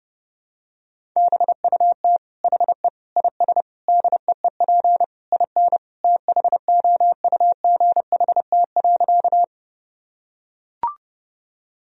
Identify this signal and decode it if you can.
Morse code, heard as a keyed tone, about 30 wpm, 700 Hz. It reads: BUT HE IS DEEP IN THOUGHT.
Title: Sentences from Top 1000 Words - Review Part 3 - 30wpm